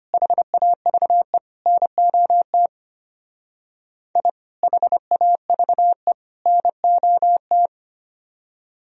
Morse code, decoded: HAVE NOT I HAVE NOT

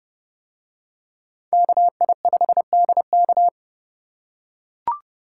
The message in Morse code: KI5DK

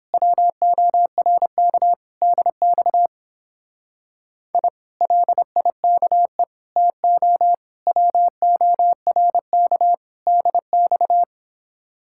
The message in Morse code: WORK DX I LIKE TO WORK DX

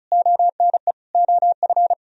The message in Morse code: ONE OF

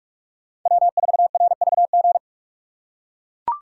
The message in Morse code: W4RVG

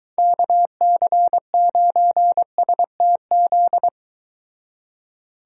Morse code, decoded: KC9STZ

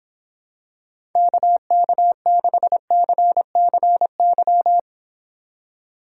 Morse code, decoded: KK6CCY

